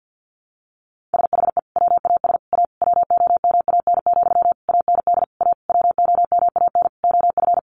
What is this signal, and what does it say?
Morse code, heard as a keyed tone, about 48 wpm, 700 Hz. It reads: SHE WAS A WOMAN, AND A WOMAN OF
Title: Sentences from Top 1400 Words 50wpm